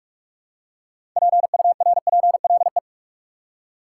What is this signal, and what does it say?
Morse code, heard as a keyed tone, about 43 wpm, 700 Hz. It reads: PURPLE
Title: Six Letter Words - 45wpm